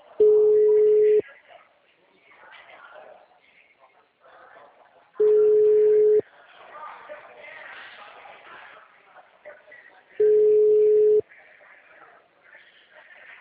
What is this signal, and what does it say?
Ringback tone (Europe, 425 Hz), with speech.